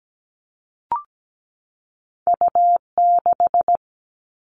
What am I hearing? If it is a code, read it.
Morse code, heard as a keyed tone, about 17 wpm, 700 Hz. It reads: U6
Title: All Cut Numbers - 2-Digit 17wpm